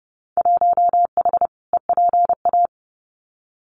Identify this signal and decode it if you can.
Morse code, heard as a keyed tone, about 30 wpm, 700 Hz. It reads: 1H EPA
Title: Winter Field Day Exchanges 30wpm